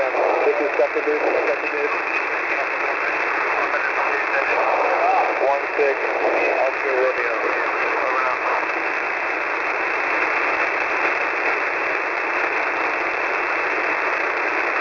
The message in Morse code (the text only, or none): none